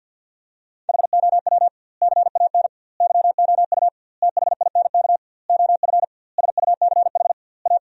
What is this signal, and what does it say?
Morse code, heard as a keyed tone, about 53 wpm, 700 Hz. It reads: HOW CAN YOU THINK OF SUCH A
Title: 2x Word-Spacing - Sentences from Top 200 Words - 50wpm